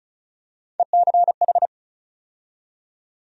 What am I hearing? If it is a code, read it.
Morse code, heard as a keyed tone, about 36 wpm, 700 Hz. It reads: ECH
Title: USA Call Sign Postfixes - 3 Letters 35wpm